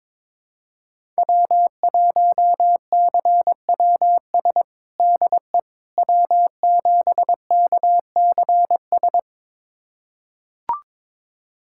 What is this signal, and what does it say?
Morse code, heard as a keyed tone, about 22 wpm, 700 Hz. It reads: W1CWS DE W7KCS
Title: USA Callsign DE Callsign - 5 Characters - 22wpm